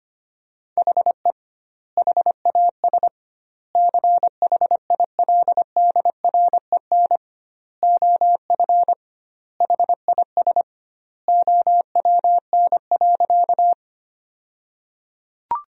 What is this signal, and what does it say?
Morse code, heard as a keyed tone, about 25 wpm, 700 Hz. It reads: HE HAS CHILDREN OF HIS OWN.
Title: 2x Word-Spacing - Sentences from Top 300 Words - 25wpm